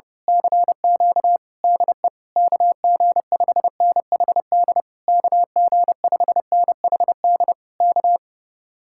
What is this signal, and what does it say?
Morse code, heard as a keyed tone, about 29 wpm, 700 Hz. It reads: CQ DE KG5NHD KG5NHD K
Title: Speed-Racing Common QSO Phrases - 30wpm